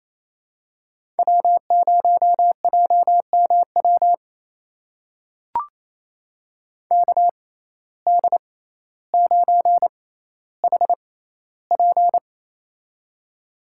Morse code, heard as a keyed tone, about 28 wpm, spaced as for 14 wpm, 700 Hz. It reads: W0JMW KD9HP
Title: USA Call Signs - 6x Character Spacing - 28wpm